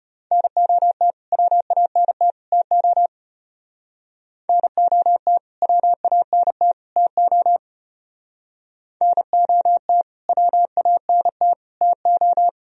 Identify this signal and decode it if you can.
Morse code, heard as a keyed tone, about 33 wpm, 700 Hz. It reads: NOT WANT TO NOT WANT TO NOT WANT TO